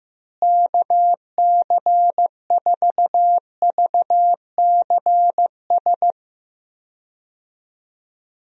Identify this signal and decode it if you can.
Morse code, heard as a keyed tone, about 15 wpm, 700 Hz. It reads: KC4VCS
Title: USA Call Signs - 6 Characters - Repeated 2x - 15wpm